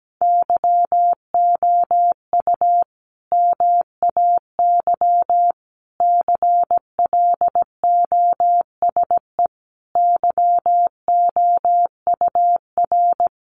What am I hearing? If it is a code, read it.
Morse code, heard as a keyed tone, about 17 wpm, 700 Hz. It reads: YOU MAY CLOSE YOUR
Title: Sentences from Top 1500 Words 17wpm